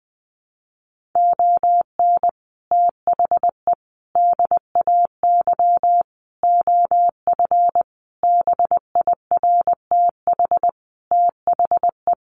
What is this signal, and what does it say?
Morse code, heard as a keyed tone, about 20 wpm, 700 Hz. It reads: ON THE DAY OF BIRTH THE
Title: Sentences from Top 1100 Words 20wpm